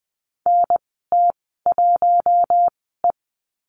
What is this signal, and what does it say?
Morse code, heard as a keyed tone, about 20 wpm, spaced as for 15 wpm, 700 Hz. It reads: NT1E
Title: 076 Lesson 13g - Letter L - Call Signs 20wpm